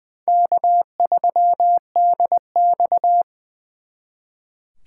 Morse code, heard as a keyed tone, about 23 wpm, 700 Hz. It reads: K3DX
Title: Speed Racing - USA 1x2 Call Signs 20wpm